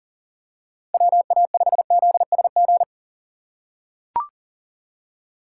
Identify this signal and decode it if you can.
Morse code, heard as a keyed tone, about 40 wpm, 700 Hz. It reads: WA5ZSG
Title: USA Call Signs - 6 Characters - 3x Character Spacing - 40wpm